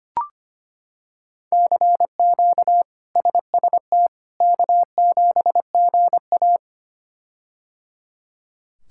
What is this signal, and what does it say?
Morse code, heard as a keyed tone, about 25 wpm, 700 Hz. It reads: CQ SST K7GA
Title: K1USN SST QSO Practice - 25wpm